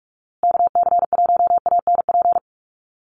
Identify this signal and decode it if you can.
Morse code, heard as a keyed tone, about 44 wpm, 700 Hz. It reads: KC1ANP
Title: USA Call Signs Repeated 2x 45wpm